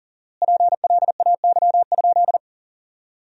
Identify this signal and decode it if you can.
Morse code, heard as a keyed tone, about 40 wpm, 700 Hz. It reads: PLAY?
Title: 4x Word-Spacing - Sentences from Top 400 Words - 40wpm